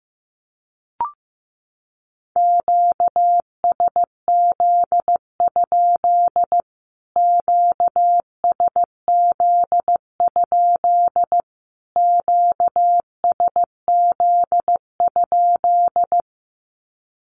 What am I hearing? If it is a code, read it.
Morse code, heard as a keyed tone, about 15 wpm, 700 Hz. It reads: QSZ? QSZ? QSZ?
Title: Q-Codes Repeated 3x 15wpm